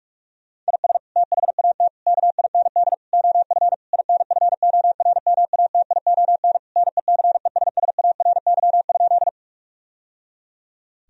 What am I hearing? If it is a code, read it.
Morse code, heard as a keyed tone, about 46 wpm, 700 Hz. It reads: IS THAT KIND OF INFORMATION NECESSARY?
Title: Sentences from Top 1000 Words - Review Part 3 - 45wpm